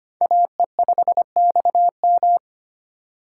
Morse code, heard as a keyed tone, about 25 wpm, 700 Hz. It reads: AE5XM
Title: USA Call Signs - 5 Characters - 2x Character Spacing - 25wpm